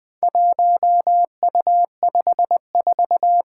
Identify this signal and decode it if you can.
Morse code, heard as a keyed tone, about 20 wpm, 700 Hz. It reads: 1U54
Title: Four Challenging Letter-Number Combinations 20wpm